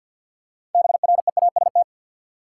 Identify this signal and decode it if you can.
Morse code, heard as a keyed tone, about 52 wpm, 700 Hz. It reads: BREAST
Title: Six Letter Words - 50wpm